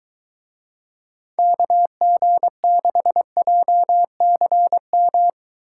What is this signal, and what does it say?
Morse code, heard as a keyed tone, about 23 wpm, 700 Hz. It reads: KG6JCM